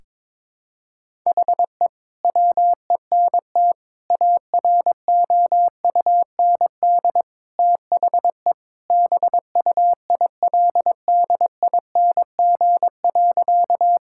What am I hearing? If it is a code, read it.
Morse code, heard as a keyed tone, about 22 wpm, 700 Hz. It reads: HE WENT AROUND THE BUILDING.
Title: Sentences from Top 1000 Words - Review Part 5 - 22wpm